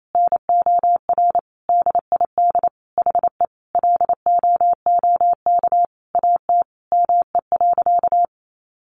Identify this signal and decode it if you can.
Morse code, heard as a keyed tone, about 28 wpm, 700 Hz. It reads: NOR DID HE LOOK AT ME.
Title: Sentences from Top 1500 Words 28wpm